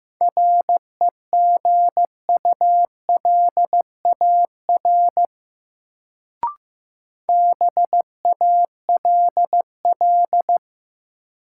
Morse code, heard as a keyed tone, about 15 wpm, 700 Hz. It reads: REGULAR BALL
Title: Sentences from Top 900 Words - New Word Drill - 15wpm